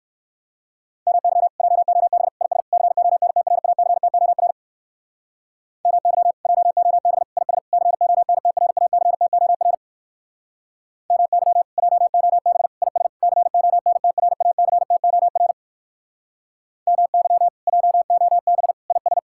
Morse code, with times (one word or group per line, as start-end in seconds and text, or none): 1.1-1.5 MY
1.6-2.3 JOB
2.4-2.6 IS
2.7-4.5 CONTRACTOR
5.9-6.3 MY
6.5-7.2 JOB
7.4-7.6 IS
7.7-9.7 CONTRACTOR
11.1-11.6 MY
11.8-12.7 JOB
12.8-13.1 IS
13.2-15.5 CONTRACTOR
16.9-17.5 MY
17.7-18.7 JOB
18.9-19.2 IS